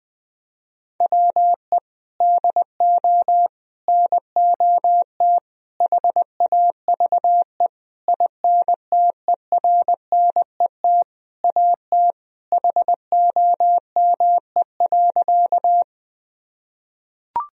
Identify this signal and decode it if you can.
Morse code, heard as a keyed tone, about 20 wpm, 700 Hz. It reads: WE DO NOT HAVE INTERNET AT HOME.
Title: Sentences from Top 1000 Words - Review Part 4 - 20wpm